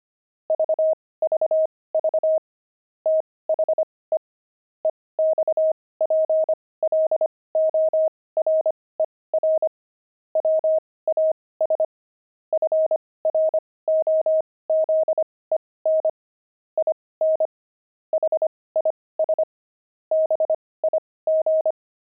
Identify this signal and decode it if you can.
Morse code, heard as a keyed tone, about 25 wpm, spaced as for 18 wpm, 630 Hz. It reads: VVV THE EXPLORER WAS FROZEN IN HIS BIG
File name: pangram-fs-18-cs-25-explorer